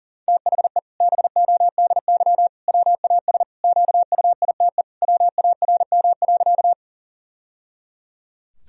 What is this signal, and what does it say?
Morse code, heard as a keyed tone, about 40 wpm, 700 Hz. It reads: THE BODY WAS QUITE WARM.
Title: Sentences from Top 1000 Words - Review Part 2 - 40wpm